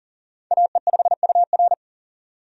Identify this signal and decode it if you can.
Morse code, heard as a keyed tone, about 37 wpm, 700 Hz. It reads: AE5UR
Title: USA Call Signs - 5 Characters - 40wpm